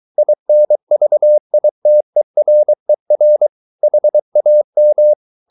Morse code, heard as a keyed tone, about 23 wpm, 600 Hz. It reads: INVITERER HAM
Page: Ferie 23 wpm